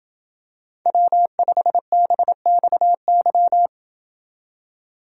Morse code, heard as a keyed tone, about 27 wpm, 700 Hz. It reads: W5BXY